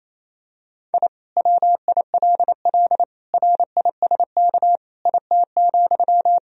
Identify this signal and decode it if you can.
Morse code, heard as a keyed tone, about 27 wpm, 700 Hz. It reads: I WILL RISK IT,
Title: Sentences from Top 1000 Words - Review Part 4 - 28wpm